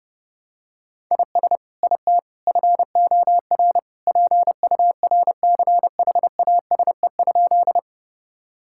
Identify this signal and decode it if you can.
Morse code, heard as a keyed tone, about 30 wpm, 700 Hz. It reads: IS IT FOR PURCHASE?
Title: Sentences from Top 1000 Words - Review Part 5 - 30wpm